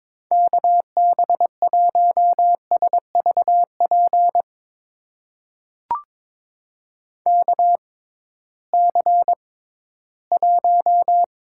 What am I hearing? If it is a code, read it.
Morse code, heard as a keyed tone, about 23 wpm, spaced as for 14 wpm, 700 Hz. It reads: KB1SVP KC1
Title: USA Call Signs - 6 Characters - 6x Character Spacing - 22wpm